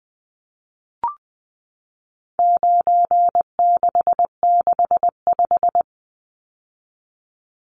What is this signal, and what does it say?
Morse code, heard as a keyed tone, about 20 wpm, 700 Hz. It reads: 9665
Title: Numbers - 4-Digit 20wpm